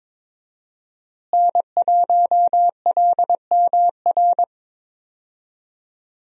Morse code, heard as a keyed tone, about 22 wpm, 700 Hz. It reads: N1LMR
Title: USA Call Signs - 5 Characters - Repeated 2x - 22wpm